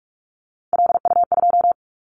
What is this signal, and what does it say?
Morse code, heard as a keyed tone, about 44 wpm, 700 Hz. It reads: LU2
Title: Callsign Prefixes for Top 50 DXCC - 45wpm